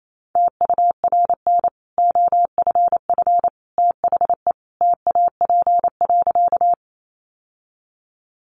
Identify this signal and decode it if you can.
Morse code, heard as a keyed tone, about 28 wpm, 700 Hz. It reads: TURN OFF THE TAP.
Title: Sentences from Top 1400 Words 28wpm